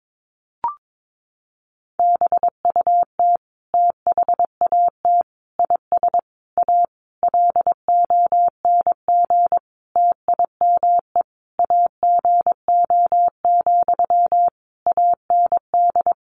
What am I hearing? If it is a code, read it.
Morse code, heard as a keyed tone, about 22 wpm, 700 Hz. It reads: BUT THAT IS A LONG TIME AGO, AND
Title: Sentences from Top 600 Words 22wpm